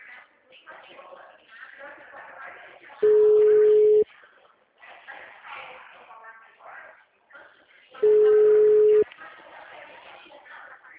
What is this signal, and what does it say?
Ringback tone (Europe, 425 Hz), with speech.
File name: out- -2045- - - .18395